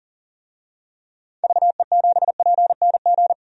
Morse code, heard as a keyed tone, about 40 wpm, 700 Hz. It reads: VE7PNG
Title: DXCC-100 Call Signs - 6 Characters - 40wpm